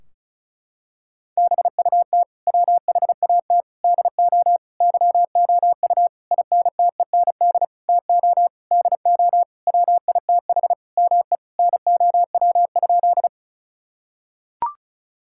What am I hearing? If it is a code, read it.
Morse code, heard as a keyed tone, about 36 wpm, 700 Hz. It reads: BUT WHAT DO YOU INTEND TO DO WITH ME NOW?
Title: Sentences from Top 1600 Words 35wpm